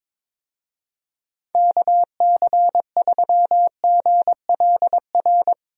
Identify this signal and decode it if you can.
Morse code, heard as a keyed tone, about 22 wpm, 700 Hz. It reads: KC3GLR
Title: USA Call Signs - 6 Characters - Repeated 3x - 22wpm